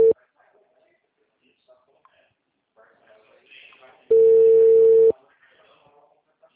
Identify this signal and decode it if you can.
Ringback tone (China, 450 Hz), with speech.